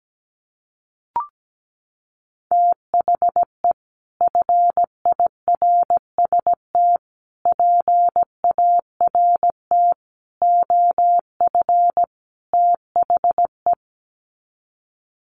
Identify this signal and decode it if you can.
Morse code, heard as a keyed tone, about 17 wpm, 700 Hz. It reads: THE FIRST PART OF THE
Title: Sets of 5 Words Encore 17wpm